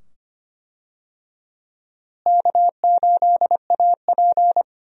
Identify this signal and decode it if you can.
Morse code, heard as a keyed tone, about 25 wpm, 700 Hz. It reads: K8AP
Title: USA Call Signs - 4 Characters - Repeated 2x - 25wpm